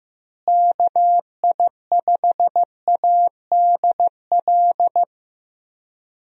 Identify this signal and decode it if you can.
Morse code, heard as a keyed tone, about 15 wpm, 700 Hz. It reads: KI5ADL
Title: USA Call Signs - 6 Characters - Repeated 3x - 15wpm